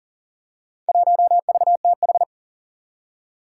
Morse code, heard as a keyed tone, about 40 wpm, 700 Hz. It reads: 1VTH
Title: Four Challenging Letter-Number Combinations 40wpm